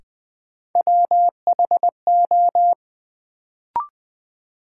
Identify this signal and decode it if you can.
Morse code, heard as a keyed tone, about 20 wpm, 700 Hz. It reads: WHO
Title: Sets of 2 Words - New Word Drill - 20wpm